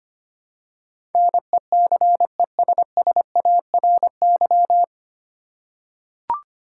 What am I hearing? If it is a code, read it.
Morse code, heard as a keyed tone, about 25 wpm, 700 Hz. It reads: NECESSARY
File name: Nine Letter Words - 25wpm